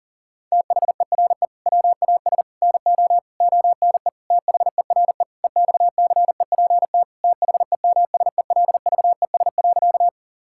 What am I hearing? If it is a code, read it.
Morse code, heard as a keyed tone, about 40 wpm, 700 Hz. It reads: THERE WAS NO ONE THERE EXCEPT THEMSELVES.
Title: Sentences from Top 1600 Words 40wpm